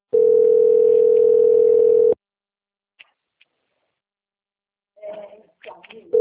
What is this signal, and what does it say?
Ringback tone (North America), 440 + 480 Hz.
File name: out- - - - - .25676